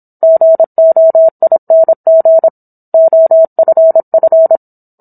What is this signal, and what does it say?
Morse code, heard as a keyed tone, about 27 wpm, 650 Hz. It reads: GOING OFF